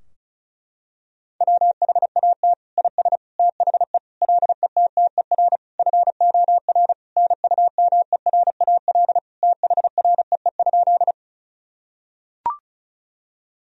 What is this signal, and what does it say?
Morse code, heard as a keyed tone, about 34 wpm, 700 Hz. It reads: WHAT IS THE LETTER FOR NUMERAL THREE?
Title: Sentences from Top 1000 Words - Review Part 2 - 35wpm